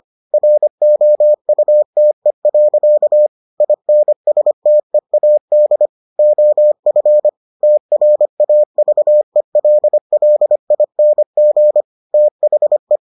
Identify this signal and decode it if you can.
Morse code, heard as a keyed tone, about 25 wpm, 600 Hz. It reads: ROUTE. INSTEAD OF TRAVELLING THE